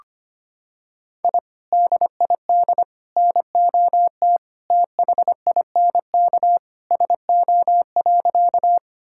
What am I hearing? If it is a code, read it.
Morse code, heard as a keyed tone, about 25 wpm, 700 Hz. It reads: I DID NOT THINK SO.
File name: practice / 25wpm / Sentences with 2-Word Phrases - 25wpm